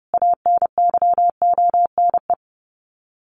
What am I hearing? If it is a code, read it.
Morse code, heard as a keyed tone, about 30 wpm, 700 Hz. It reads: ANYONE
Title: Sets of 5 Words Encore 30wpm